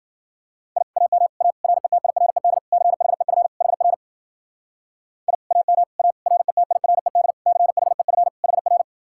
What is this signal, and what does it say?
Morse code, heard as a keyed tone, about 52 wpm, 700 Hz. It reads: I AM A RETIRED CHEF HR I AM A RETIRED CHEF HR